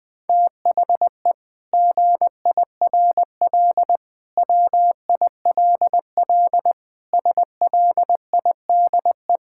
Morse code, heard as a keyed tone, about 20 wpm, 700 Hz. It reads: THE GIRL WILL SLIDE
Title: Sentences from Top 1600 Words 20wpm